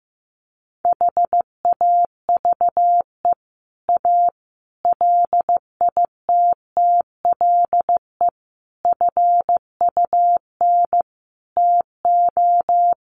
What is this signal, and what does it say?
Morse code, heard as a keyed tone, about 15 wpm, 700 Hz. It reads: HAVE A LITTLE FUN TO
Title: Sentences from Top 700 Words 15wpm